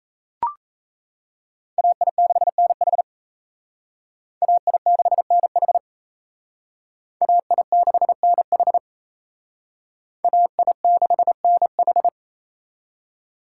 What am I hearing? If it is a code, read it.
Morse code, heard as a keyed tone, about 35 wpm, 700 Hz. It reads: AI6NH AI6NH AI6NH AI6NH